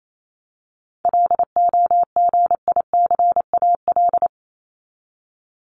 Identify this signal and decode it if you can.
Morse code, heard as a keyed tone, about 28 wpm, 700 Hz. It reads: LOGICAL
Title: Top 1501-1750 Words 28wpm